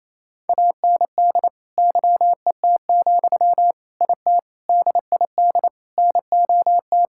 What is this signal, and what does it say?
Morse code, heard as a keyed tone, about 28 wpm, 700 Hz. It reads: AND YET, IT DID NOT
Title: Sentences from Top 1600 Words 28wpm